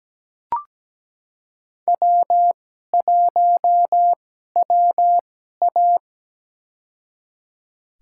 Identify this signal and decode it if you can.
Morse code, heard as a keyed tone, about 17 wpm, spaced as for 12 wpm, 700 Hz. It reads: W1WA
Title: USA Call Signs - 4 Characters - 2x Character Spacing - 17wpm